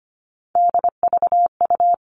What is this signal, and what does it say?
Morse code, heard as a keyed tone, about 25 wpm, 700 Hz. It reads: DVU
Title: All Cut Numbers - 3-Digit 25wpm